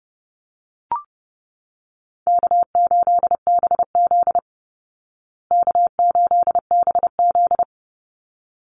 Morse code, heard as a keyed tone, about 30 wpm, 700 Hz. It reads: K8BZ K8BZ
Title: Easy Call Signs Repeated 2x - 30wpm